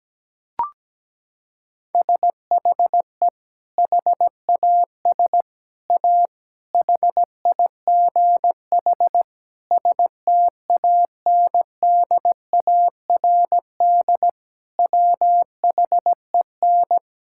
Morse code, heard as a keyed tone, about 17 wpm, 700 Hz. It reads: SHE HAS A HIGH STANDARD WHEN
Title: Sentences from Top 1000 Words - Review Part 6 - 17wpm